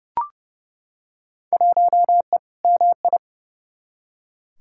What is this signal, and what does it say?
Morse code, heard as a keyed tone, about 31 wpm, 700 Hz. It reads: 1E MI
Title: Field Day Exchanges - 30wpm